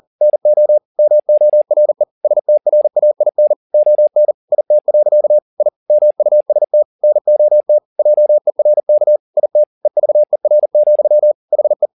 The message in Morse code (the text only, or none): NO MORE STRAIN ON IT. I MUST NOT JERK IT EVER, HE